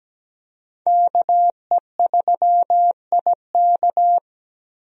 Morse code, heard as a keyed tone, about 17 wpm, 700 Hz. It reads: KE3IK
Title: USA Call Signs - 5 Characters - Repeated 3x - 17wpm